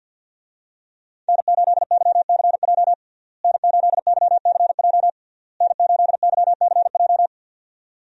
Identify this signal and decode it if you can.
Morse code, heard as a keyed tone, about 53 wpm, 700 Hz. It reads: N8YCJ N8YCJ N8YCJ
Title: USA Call Signs - 5 Characters - Repeated 3x - 50wpm